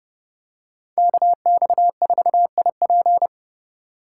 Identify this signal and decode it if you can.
Morse code, heard as a keyed tone, about 28 wpm, 700 Hz. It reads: KX4IP